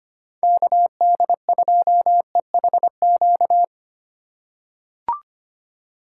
Morse code, heard as a keyed tone, about 25 wpm, 700 Hz. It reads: KD2EHQ
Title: USA Call Signs - 6 Characters - 3x Character Spacing - 25wpm